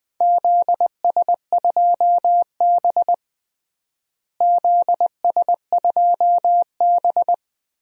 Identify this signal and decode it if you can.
Morse code, heard as a keyed tone, about 20 wpm, 700 Hz. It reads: ZS2B ZS2B
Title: DXCC-100 Call Signs - 4 Characters - Repeated 2x - 20wpm